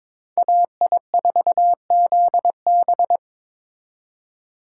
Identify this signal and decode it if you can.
Morse code, heard as a keyed tone, about 26 wpm, 700 Hz. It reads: AI4ZB
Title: Speed Racing - USA 2x2 Call Signs 22wpm